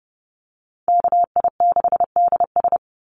Morse code, heard as a keyed tone, about 32 wpm, 700 Hz. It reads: KI6DS
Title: NA SOTA Chaser Callsigns 30wpm